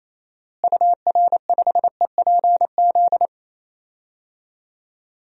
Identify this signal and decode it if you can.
Morse code, heard as a keyed tone, about 28 wpm, 700 Hz. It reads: UR5EPZ
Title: DXCC-100 Call Signs - 6 Characters - Repeated 2x - 28wpm